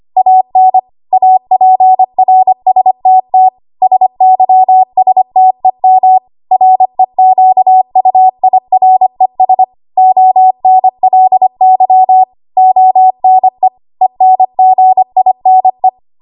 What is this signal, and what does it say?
Morse code, heard as a keyed tone, about 25 wpm, 750 Hz. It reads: AN APRSTT SYSTEM REQUIRES ONLY ONE ENGINE
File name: Morse / 25WPM